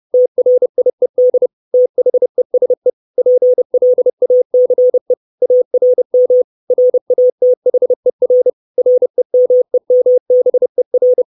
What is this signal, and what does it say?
Morse code, heard as a keyed tone, about 30 wpm, 500 Hz. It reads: TRIED THESE PLACE ARM RATHER REMEMBER
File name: 632 Words 30WPM